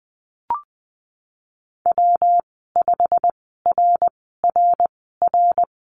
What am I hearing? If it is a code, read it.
Morse code, heard as a keyed tone, about 20 wpm, spaced as for 15 wpm, 700 Hz. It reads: W5RRR
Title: 147 Lesson 24g - Letter y - Call Signs 20wpm